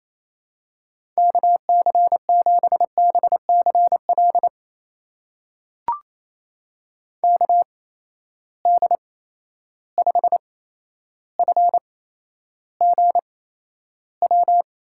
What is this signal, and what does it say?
Morse code, heard as a keyed tone, about 28 wpm, spaced as for 12 wpm, 700 Hz. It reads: KC7BCL KD5FGW
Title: USA Call Signs - 6 Characters - 8x Character Spacing - 28wpm